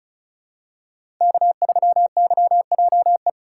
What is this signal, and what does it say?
Morse code, heard as a keyed tone, about 36 wpm, 700 Hz. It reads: K3YJE
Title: USA Call Signs - 5 Characters - 2x Character Spacing - 35wpm